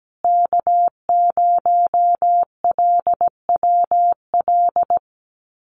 Morse code, heard as a keyed tone, about 17 wpm, 700 Hz. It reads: K0LWL
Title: USA Call Signs Repeated 3x 17wpm